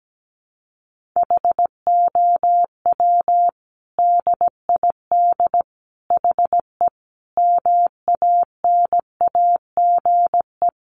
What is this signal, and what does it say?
Morse code, heard as a keyed tone, about 17 wpm, 700 Hz. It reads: HOW DID HE MANAGE
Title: Sentences from Top 1400 Words 17wpm